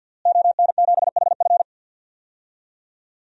Morse code, heard as a keyed tone, about 52 wpm, 700 Hz. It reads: ON7HF
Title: DXCC-100 Call Signs - 5 Characters - Repeated 3x - 50wpm